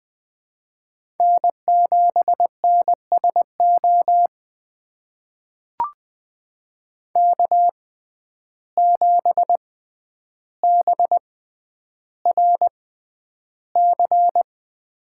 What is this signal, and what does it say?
Morse code, heard as a keyed tone, about 20 wpm, spaced as for 10 wpm, 700 Hz. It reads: N7NSO K7BRC
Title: USA Call Signs - 5 Characters - 6x Character Spacing - 20wpm